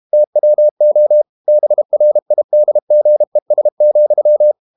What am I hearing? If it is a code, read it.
Morse code, heard as a keyed tone, about 32 wpm, 600 Hz. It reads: TWO BRIDGES,
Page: Never 31wpm | CW med Gnister